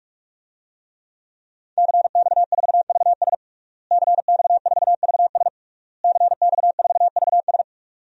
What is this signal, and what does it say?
Morse code, heard as a keyed tone, about 46 wpm, 700 Hz. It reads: CX4VS CX4VS CX4VS
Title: DXCC-100 Call Signs - 5 Characters - Repeated 3x - 45wpm